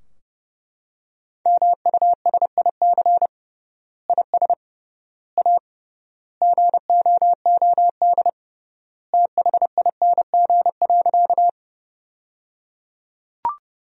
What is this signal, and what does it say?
Morse code, heard as a keyed tone, about 30 wpm, 700 Hz. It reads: MUSIC IS A GOOD THING.
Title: 3x Word-Spacing - Sentences from Top 300 Words - 30wpm